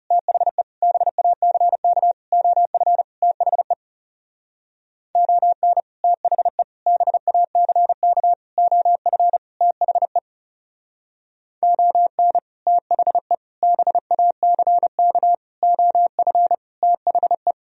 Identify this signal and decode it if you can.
Morse code, heard as a keyed tone, about 37 wpm, 700 Hz. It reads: THE BACK OF THE ON THE BACK OF THE ON THE BACK OF THE
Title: Speed-Racing - Sets of 5 Words 30wpm